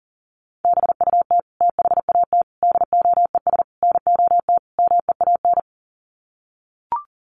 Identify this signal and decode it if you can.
Morse code, heard as a keyed tone, about 40 wpm, 700 Hz. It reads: BUT THAT DOES NOT MEAN
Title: Sets of 5 Words 40wpm